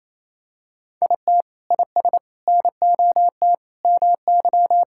Morse code, heard as a keyed tone, about 28 wpm, 700 Hz. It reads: IT IS NOT MY
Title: Sentences from Top 1000 Words - Review Part 5 - 28wpm